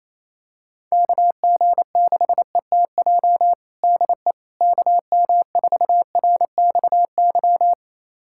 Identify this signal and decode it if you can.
Morse code, heard as a keyed tone, about 28 wpm, 700 Hz. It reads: KG6ETJ DE KM4RXY
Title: USA Callsign DE Callsign 28wpm